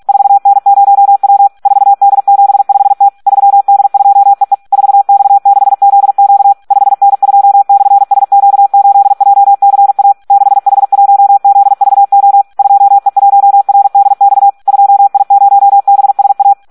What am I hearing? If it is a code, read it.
Morse code, heard as a keyed tone, about 46 wpm, 800 Hz. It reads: 4N0W 4D75T 3D2EE 4X6ZQ 5N2/SQ8JCA 6H1ZVO 2E1RDX 2I0BSA
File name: Audio practica CW / SD 1 46